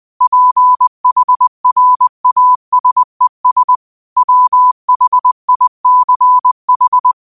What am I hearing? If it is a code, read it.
Morse code, heard as a keyed tone, about 20 wpm, 1000 Hz. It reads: PHRASES WHICH